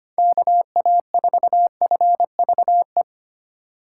Morse code, heard as a keyed tone, about 25 wpm, 700 Hz. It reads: KA4FVE
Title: USA Call Signs - 6 Characters - 2x Character Spacing - 25wpm